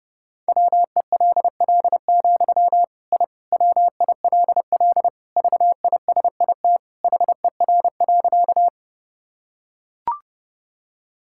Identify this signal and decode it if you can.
Morse code, heard as a keyed tone, about 30 wpm, 700 Hz. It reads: WELL, I WILL VISIT HER.
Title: Sentences from Top 1000 Words - Review Part 6 - 30wpm